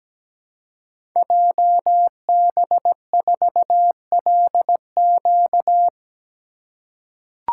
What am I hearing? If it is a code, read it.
Morse code, heard as a keyed tone, about 17 wpm, 700 Hz. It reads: JB4LQ
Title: Code Groups - 5 Letter:Numbers - 17wpm